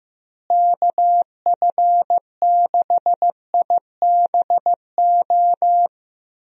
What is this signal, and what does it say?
Morse code, heard as a keyed tone, about 15 wpm, 700 Hz. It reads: KF6IBO
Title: USA Call Signs - 6 Characters - Repeated 2x - 15wpm